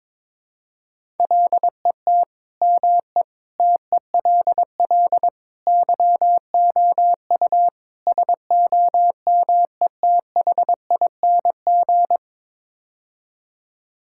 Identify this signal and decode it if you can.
Morse code, heard as a keyed tone, about 22 wpm, 700 Hz. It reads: LET ME TELL YOU SOMETHING
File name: practice / 22wpm / Build A Phrase - 5-Word Phrases - 22wpm